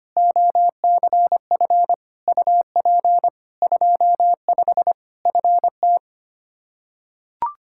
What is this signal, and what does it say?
Morse code, heard as a keyed tone, about 25 wpm, 700 Hz. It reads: OCF UP 25 FT
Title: Practice QSO - Exchange 2 - Info - 25wpm